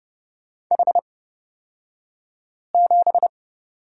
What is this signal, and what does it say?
Morse code, heard as a keyed tone, about 29 wpm, spaced as for 5 wpm, 700 Hz. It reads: H7
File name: Lesson 36d - BT - Two Characters 30wpm